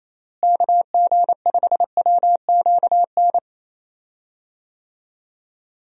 Morse code, heard as a keyed tone, about 28 wpm, 700 Hz. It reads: KG5WQN
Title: USA Call Signs - 6 Characters - Repeated 3x - 28wpm